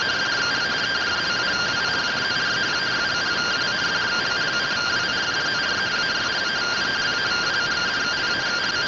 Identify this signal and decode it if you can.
Two-tone teleprinter signal, about 170 Hz shift.